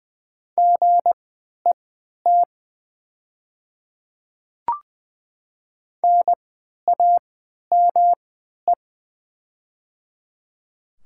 Morse code, heard as a keyed tone, about 20 wpm, spaced as for 11 wpm, 700 Hz. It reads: GET NAME
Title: 3x Character Spacing - Top 300 Words - 20wpm